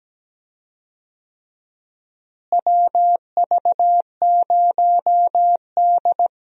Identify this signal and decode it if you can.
Morse code, heard as a keyed tone, about 17 wpm, 700 Hz. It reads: WV0D
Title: USA Call Signs - 4 Characters - Repeated 3x - 17wpm